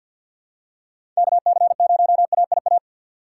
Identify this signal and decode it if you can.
Morse code, heard as a keyed tone, about 53 wpm, 700 Hz. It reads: KC0AIA
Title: USA Call Signs - 6 Characters - 3x Character Spacing - 50wpm